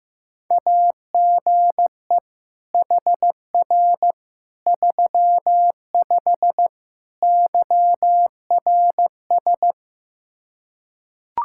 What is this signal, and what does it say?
Morse code, heard as a keyed tone, about 15 wpm, 700 Hz. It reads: AGE HR 35 YRS
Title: Practice QSO - Exchange 3 - More Info - 15wpm